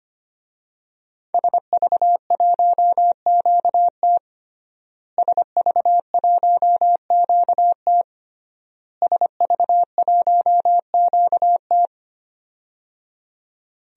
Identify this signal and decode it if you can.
Morse code, heard as a keyed tone, about 25 wpm, 700 Hz. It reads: SV1QT SV1QT SV1QT
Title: DXCC-100 Call Signs - 5 Characters - Repeated 3x - 25wpm